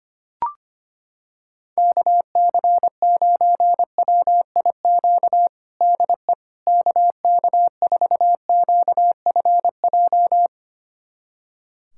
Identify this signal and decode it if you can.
Morse code, heard as a keyed tone, about 25 wpm, 700 Hz. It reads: KC9WIQ DE KK4QFJ
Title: USA Callsign DE Callsign - 6 Characters - 25wpm